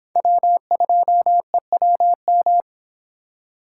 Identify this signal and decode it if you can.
Morse code, heard as a keyed tone, about 26 wpm, 700 Hz. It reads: W2EWM